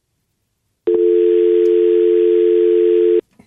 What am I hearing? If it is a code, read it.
Dial tone (North America), 350 + 440 Hz.